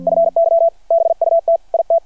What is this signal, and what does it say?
Morse code, heard as a keyed tone, about 53 wpm, 660 Hz. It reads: FY BUT IT
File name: archived / CRSnet